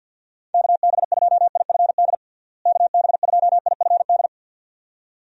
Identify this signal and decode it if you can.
Morse code, heard as a keyed tone, about 47 wpm, 700 Hz. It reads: KB2IFD KB2IFD
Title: USA Call Signs - 6 Characters - Repeated 2x - 50wpm